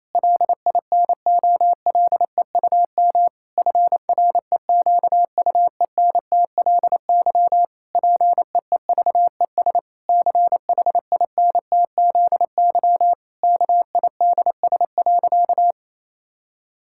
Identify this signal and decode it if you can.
Morse code, heard as a keyed tone, about 28 wpm, 700 Hz. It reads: LINOLEUM FREQUENTLY PEEVES CHINTZY KIDS.
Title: Pangram Challenge v2 - 28wpm